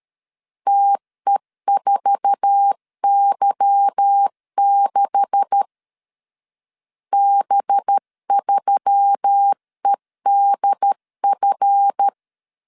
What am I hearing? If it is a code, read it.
Morse code, heard as a keyed tone, about 13 wpm, 800 Hz. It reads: TE4Y6 B3EDF
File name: LEZIONI / 9 286 12wpm 2305